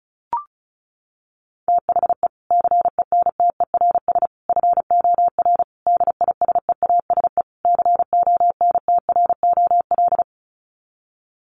Morse code, heard as a keyed tone, about 34 wpm, 700 Hz. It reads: THE CENTERS FOR DISEASE CONTROL
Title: Sets of 5 Words Encore2 35wpm